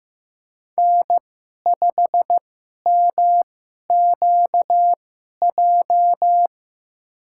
Morse code, heard as a keyed tone, about 15 wpm, spaced as for 11 wpm, 700 Hz. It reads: N5MQJ
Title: USA Call Signs - 2x Character Spacing - 15wpm